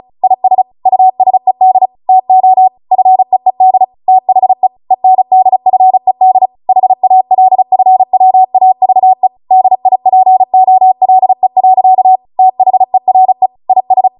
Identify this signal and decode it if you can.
Morse code, heard as a keyed tone, about 36 wpm, 750 Hz. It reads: IS USED TO FEED THE ENDFED HALFWAVE DIPOLE. THERE IS